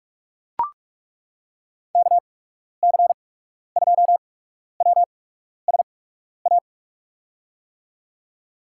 Morse code, heard as a keyed tone, about 42 wpm, spaced as for 12 wpm, 700 Hz. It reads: KC2WSA
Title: USA Call Signs - 6 Characters - 8x Character Spacing - 45wpm